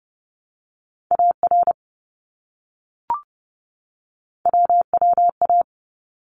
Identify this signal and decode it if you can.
Morse code, heard as a keyed tone, about 30 wpm, 700 Hz. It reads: AR WWA
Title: ARRL Sections 30wpm